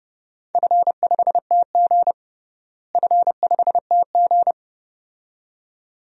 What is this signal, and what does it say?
Morse code, heard as a keyed tone, about 30 wpm, 700 Hz. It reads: F5TG F5TG
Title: DXCC-100 Call Signs - 4 Characters - Repeated 2x - 30wpm